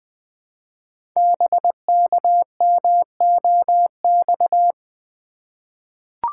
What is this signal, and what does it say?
Morse code, heard as a keyed tone, about 20 wpm, 700 Hz. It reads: BKMOX
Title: Code Groups - 5 Letters - 20wpm